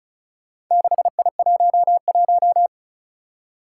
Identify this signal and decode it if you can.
Morse code, heard as a keyed tone, about 36 wpm, 700 Hz. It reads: 6I11
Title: Four Challenging Letter-Number Combinations 35wpm